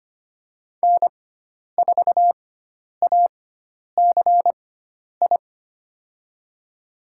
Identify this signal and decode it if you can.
Morse code, heard as a keyed tone, about 25 wpm, spaced as for 10 wpm, 700 Hz. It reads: N4ACI
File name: Lesson 16g - Letter c - Call Signs 25wpm